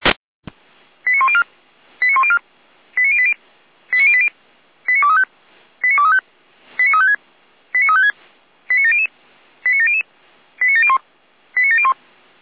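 Five-tone selective call (ZVEI), heard as quick runs of five tones, tones of about 70 ms. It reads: 89104 89104 89080 89080 89235 89235 89356 89356 89700 89700 89701 89701
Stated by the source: ZVEI five-tone selective call received on an SDR.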